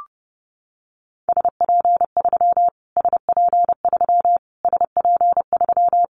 Morse code, heard as a keyed tone, about 30 wpm, 700 Hz. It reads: SP3 SP3 SP3
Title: Callsign Prefixes for Top 25 DXCC - Repeated 3x - 30wpm